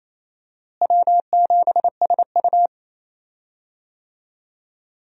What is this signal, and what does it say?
Morse code, heard as a keyed tone, about 28 wpm, 700 Hz. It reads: W7SU
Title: USA Call Signs - 4 Characters - Repeated 3x - 28wpm